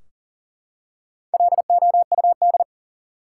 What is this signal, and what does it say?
Morse code, heard as a keyed tone, about 40 wpm, 700 Hz. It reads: LOUD
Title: Four Letter Words - 40wpm